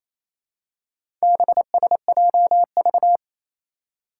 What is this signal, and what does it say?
Morse code, heard as a keyed tone, about 28 wpm, 700 Hz. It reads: BSJV
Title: Four Challenging Letter-Number Combinations 28wpm